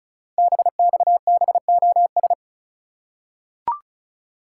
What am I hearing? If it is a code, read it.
Morse code, heard as a keyed tone, about 36 wpm, 700 Hz. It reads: BXBOS
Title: Code Groups - 5 Letters - 35wpm